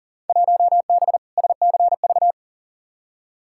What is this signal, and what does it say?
Morse code, heard as a keyed tone, about 40 wpm, 700 Hz. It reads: 1B SCV
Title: Field Day Exchanges - 40wpm